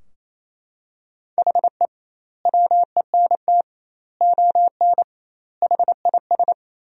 Morse code, heard as a keyed tone, about 28 wpm, 700 Hz. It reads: HE WENT ON HIS
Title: 2x Word-Spacing - Sentences from Top 200 Words - 28wpm